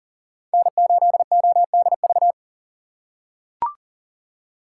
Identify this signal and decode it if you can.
Morse code, heard as a keyed tone, about 40 wpm, 700 Hz. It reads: N8ODV
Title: USA Call Signs - 2x Character Spacing - 40wpm